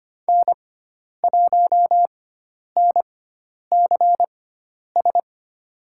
Morse code, heard as a keyed tone, about 25 wpm, spaced as for 10 wpm, 700 Hz. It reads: N1NCS
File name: Lesson 16g - Letter c - Call Signs 25wpm